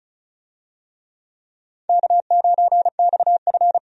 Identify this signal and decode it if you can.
Morse code, heard as a keyed tone, about 34 wpm, 700 Hz. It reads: K9XF
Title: USA Call Signs - 4 Characters - 35wpm